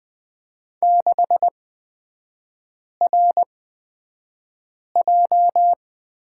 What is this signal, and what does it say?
Morse code, heard as a keyed tone, about 20 wpm, spaced as for 5 wpm, 690 Hz. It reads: 6RJ
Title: 229 Lesson 37e - Letter x - Three Characters 20wpm